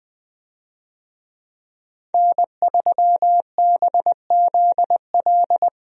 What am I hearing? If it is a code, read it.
Morse code, heard as a keyed tone, about 20 wpm, 700 Hz. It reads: N3BZL
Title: USA Call Signs - 5 Characters - Repeated 3x - 20wpm